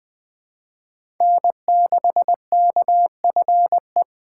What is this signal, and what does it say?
Morse code, heard as a keyed tone, about 20 wpm, 700 Hz. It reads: N6KFE
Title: USA Call Signs - 5 Characters - 4x Character Spacing - 20wpm